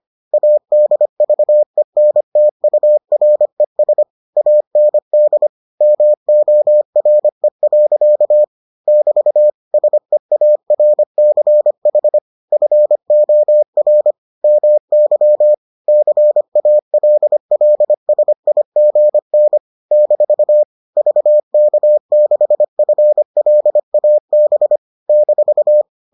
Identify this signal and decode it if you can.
Morse code, heard as a keyed tone, about 25 wpm, 600 Hz. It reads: ADVENTURES AND MORE. <BT> SEARCH FOR MY CALLSIGN - VK6FLAB -